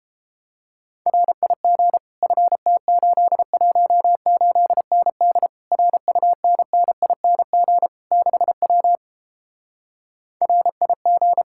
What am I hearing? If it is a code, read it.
Morse code, heard as a keyed tone, about 29 wpm, 700 Hz. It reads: RIG FT818ND RUNNING 6W RIG